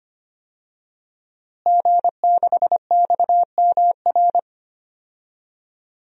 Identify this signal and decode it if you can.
Morse code, heard as a keyed tone, about 25 wpm, 700 Hz. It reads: G6XMR
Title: DXCC-100 Call Signs - 5 Characters - Repeated 2x - 25wpm